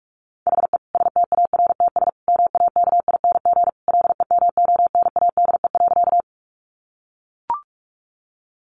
Morse code, heard as a keyed tone, about 44 wpm, 700 Hz. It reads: HE STARTS MAKING LEMONADE.
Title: Additional Easy Phrases 45wpm